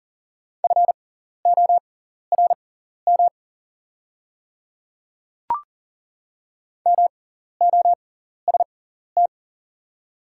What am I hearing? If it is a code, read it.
Morse code, heard as a keyed tone, about 40 wpm, spaced as for 14 wpm, 700 Hz. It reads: FORM MOST
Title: 6x Character Spacing - Top 200 Words - 40wpm